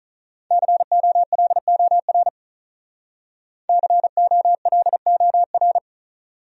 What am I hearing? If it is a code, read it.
Morse code, heard as a keyed tone, about 44 wpm, 700 Hz. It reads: COLOR COLOR
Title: Speed-Racing Top 300 Words 35wpm